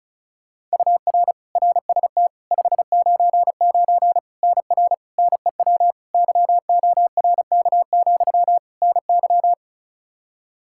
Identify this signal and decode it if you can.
Morse code, heard as a keyed tone, about 36 wpm, 700 Hz. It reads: UR RST 599 NR NEW YORK, NY
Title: K1USN SST QSO Practice 35wpm